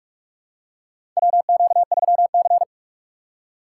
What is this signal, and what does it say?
Morse code, heard as a keyed tone, about 47 wpm, 700 Hz. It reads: WQ3C
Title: USA Call Signs - 4 Characters - 6x Character Spacing - 45wpm